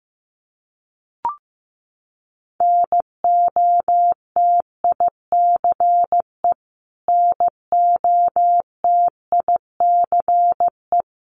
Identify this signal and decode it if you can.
Morse code, heard as a keyed tone, about 15 wpm, 700 Hz. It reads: NOTICE NOTICE
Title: Top 500 Words Repeated 15wpm